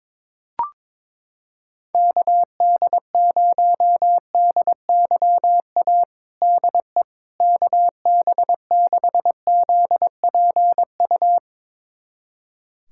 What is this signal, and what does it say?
Morse code, heard as a keyed tone, about 21 wpm, 700 Hz. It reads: KD0DYA DE KB6ZPU
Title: USA Callsign DE Callsign 22wpm